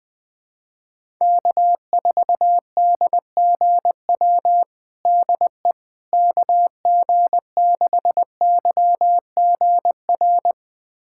Morse code, heard as a keyed tone, about 20 wpm, 690 Hz. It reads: K4DGW DE KG6YGR
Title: USA Callsign DE Callsign 20wpm